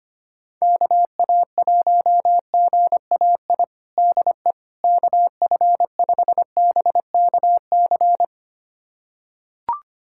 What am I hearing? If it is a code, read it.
Morse code, heard as a keyed tone, about 25 wpm, 700 Hz. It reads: KA1GAI DE KF5BKC
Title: USA Callsign DE Callsign - 6 Characters - 25wpm